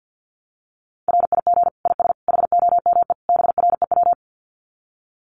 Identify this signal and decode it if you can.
Morse code, heard as a keyed tone, about 48 wpm, 700 Hz. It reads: RIG IS HOME BREW
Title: Common QSO Phrases 50wpm